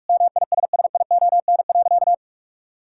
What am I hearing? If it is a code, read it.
Morse code, heard as a keyed tone, about 46 wpm, 700 Hz. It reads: MISSION.
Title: Sentences from Top 1600 Words 45wpm